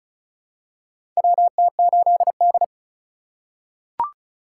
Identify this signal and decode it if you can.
Morse code, heard as a keyed tone, about 36 wpm, 700 Hz. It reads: WT8D
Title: USA Call Signs - 4 Characters - 2x Character Spacing - 35wpm